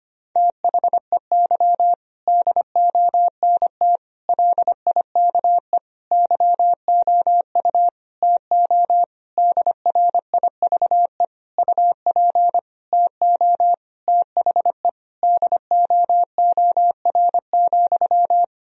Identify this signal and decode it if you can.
Morse code, heard as a keyed tone, about 25 wpm, 700 Hz. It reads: THEY DONT LIKE YOU TO DRIVE UP TO THE DOOR,
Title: Sentences from Top 1000 Words - Review Part 2 - 25wpm